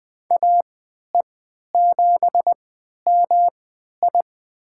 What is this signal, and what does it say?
Morse code, heard as a keyed tone, about 21 wpm, spaced as for 12 wpm, 700 Hz. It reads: AE7MI
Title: USA Call Signs - 5 Characters - 3x Character Spacing - 20wpm